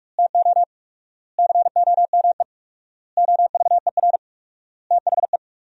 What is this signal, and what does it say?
Morse code, heard as a keyed tone, about 46 wpm, 700 Hz. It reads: TO COME OVER THE
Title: 4x Word-Spacing - Sentences from Top 300 Words - 45wpm